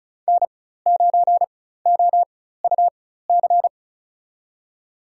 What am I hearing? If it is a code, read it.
Morse code, heard as a keyed tone, about 34 wpm, spaced as for 16 wpm, 700 Hz. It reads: N9OUC
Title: USA Call Signs - 4x Character Spacing - 35wpm